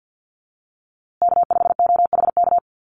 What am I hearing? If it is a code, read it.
Morse code, heard as a keyed tone, about 50 wpm, 700 Hz. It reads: K5OHK
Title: USA Call Signs Repeated 2x 50wpm